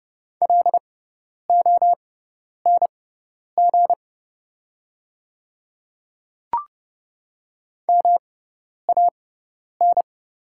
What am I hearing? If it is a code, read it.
Morse code, heard as a keyed tone, about 30 wpm, spaced as for 10 wpm, 700 Hz. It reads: LONG MAN
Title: 6x Character Spacing - Top 200 Words - 30wpm